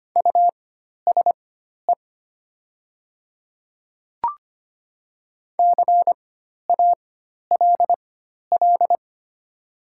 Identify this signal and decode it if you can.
Morse code, heard as a keyed tone, about 25 wpm, spaced as for 12 wpm, 700 Hz. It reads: USE CALL
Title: 4x Character Spacing - Top 100 Words - 25wpm